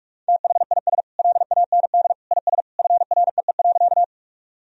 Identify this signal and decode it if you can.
Morse code, heard as a keyed tone, about 46 wpm, 700 Hz. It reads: THIS LAND IS FREE.
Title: Sentences from Top 1000 Words - Review Part 1 - 45wpm